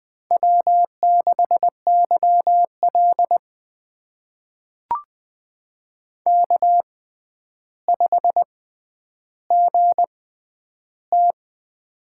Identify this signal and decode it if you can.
Morse code, heard as a keyed tone, about 20 wpm, spaced as for 10 wpm, 700 Hz. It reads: W6YL K5GT
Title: USA Call Signs - 4 Characters - 6x Character Spacing - 20wpm